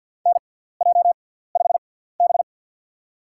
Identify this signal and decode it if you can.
Morse code, heard as a keyed tone, about 50 wpm, spaced as for 17 wpm, 700 Hz. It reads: NJ5B
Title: USA Call Signs - 6x Character Spacing - 50wpm